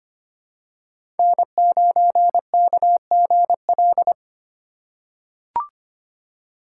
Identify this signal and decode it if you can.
Morse code, heard as a keyed tone, about 25 wpm, 700 Hz. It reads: N9KGL
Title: USA Call Signs - 4x Character Spacing - 25wpm